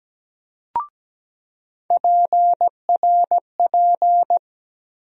Morse code, heard as a keyed tone, about 17 wpm, 700 Hz. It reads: PRP
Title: USA Call Sign Postfixes - 3 Letters 17wpm